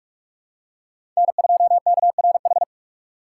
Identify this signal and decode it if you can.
Morse code, heard as a keyed tone, about 46 wpm, 700 Hz. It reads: N2KRH
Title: USA Call Signs - 6x Character Spacing - 45wpm